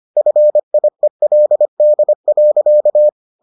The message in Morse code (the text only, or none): FIELD.